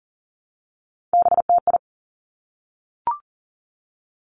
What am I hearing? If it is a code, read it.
Morse code, heard as a keyed tone, about 41 wpm, 700 Hz. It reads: BTI
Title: Three Challenging Letter-Number Combinations 40wpm